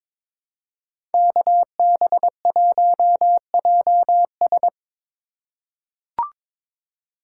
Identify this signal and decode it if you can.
Morse code, heard as a keyed tone, about 22 wpm, 700 Hz. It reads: KB1JS
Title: USA Call Signs - 5 Characters - 3x Character Spacing - 22wpm